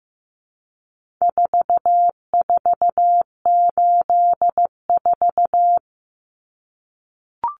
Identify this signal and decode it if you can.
Morse code, heard as a keyed tone, about 15 wpm, 700 Hz. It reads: 4484
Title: Common Cut Numbers - 4-Digit 15wpm